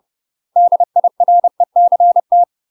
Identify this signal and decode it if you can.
Morse code, heard as a keyed tone, about 30 wpm, 700 Hz. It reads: DIRECT
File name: UK MASTER QSOs TRAINING / 30WPM / UK MASTER QSOs 0020